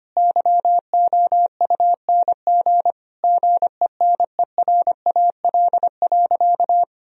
Morse code, heard as a keyed tone, about 25 wpm, 700 Hz. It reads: YOUNG GENERAL.
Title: Sentences from Top 1000 Words - Review Part 3 - 25wpm